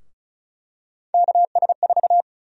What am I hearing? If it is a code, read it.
Morse code, heard as a keyed tone, about 36 wpm, 700 Hz. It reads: KS4
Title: USA Call Sign Prefixes - 35wpm